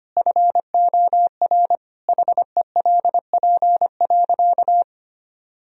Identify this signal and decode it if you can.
Morse code, heard as a keyed tone, about 25 wpm, 700 Hz. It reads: FOR HELP.
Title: Sentences from Top 1000 Words - Review Part 3 - 25wpm